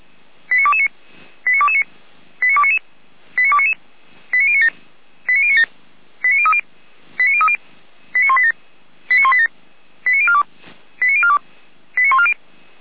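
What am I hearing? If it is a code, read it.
Five-tone selective call (ZVEI), heard as quick runs of five tones, tones of about 70 ms. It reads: 89229 89229 89299 89299 89097 89097 89030 89030 89187 89187 89042 89042 89140
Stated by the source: ZVEI five-tone selective call, code 89277, received by SDR; Feuerwehr Orlamünde fire brigade.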